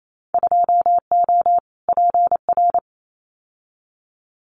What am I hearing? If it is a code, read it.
Morse code, heard as a keyed tone, about 28 wpm, 700 Hz. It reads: 2O PR
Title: Winter Field Day Exchanges Encore 28wpm